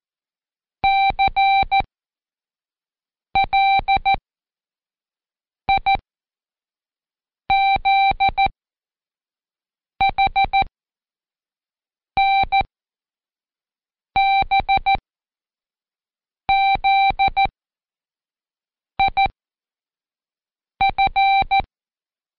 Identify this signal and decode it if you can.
Morse code, heard as a keyed tone, about 14 wpm, spaced as for 5 wpm, 780 Hz. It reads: CLIZHNBZIF
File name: Code Quick / AudioDownld1 / Seg49-Day5-CheckTest 5